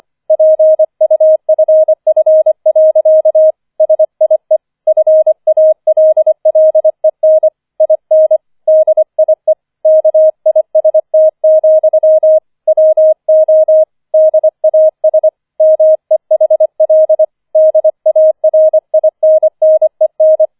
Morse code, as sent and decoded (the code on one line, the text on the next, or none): .--. ..- ..-. ..-. .-.-.- / ... .. / ..-. .- .-.. .-.. -. / .. -. / -.. .. / -.- .. ... - --..-- / .-- --- / -.. .- ... / -- . .... .-.. / -.. .- .-. .. -. -. -.
PUFF. SIE FALLEN IN DIE KIST, WO DAS MEHL DARINNEN